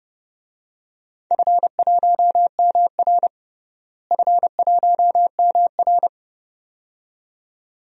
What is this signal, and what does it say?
Morse code, heard as a keyed tone, about 30 wpm, 700 Hz. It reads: F1MR F1MR
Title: DXCC-100 Call Signs - 4 Characters - Repeated 2x - 30wpm